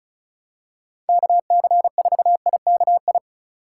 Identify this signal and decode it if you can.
Morse code, heard as a keyed tone, about 34 wpm, 700 Hz. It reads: KC4IKI
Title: USA Call Signs - 6 Characters - 2x Character Spacing - 35wpm